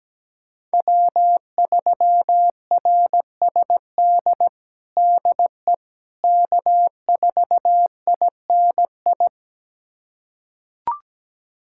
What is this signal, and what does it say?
Morse code, heard as a keyed tone, about 17 wpm, 700 Hz. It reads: W3RSD DE K4INI
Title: USA Callsign DE Callsign - 5 Characters - 17wpm